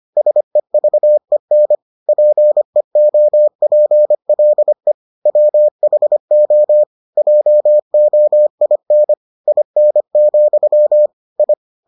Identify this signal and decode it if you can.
Morse code, heard as a keyed tone, about 25 wpm, 600 Hz. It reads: SEVEN PEOPLE WHO JOIN IN, I